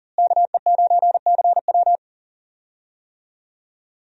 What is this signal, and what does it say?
Morse code, heard as a keyed tone, about 40 wpm, 700 Hz. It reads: KE9CW
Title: USA Call Signs - 5 Characters - Repeated 3x - 40wpm